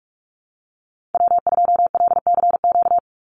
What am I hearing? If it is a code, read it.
Morse code, heard as a keyed tone, about 44 wpm, 700 Hz. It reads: W2LCQ
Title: Call Signs Repeated 3x - 45wpm